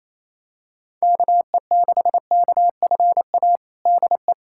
Morse code, heard as a keyed tone, about 28 wpm, 700 Hz. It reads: KE6KFA DE
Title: USA Callsign DE Callsign - 6 Characters - 28wpm